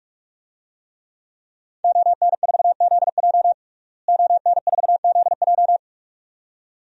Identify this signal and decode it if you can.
Morse code, heard as a keyed tone, about 46 wpm, 700 Hz. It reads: ON4ZJ ON4ZJ
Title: DXCC-100 Call Signs - 5 Characters - Repeated 2x - 45wpm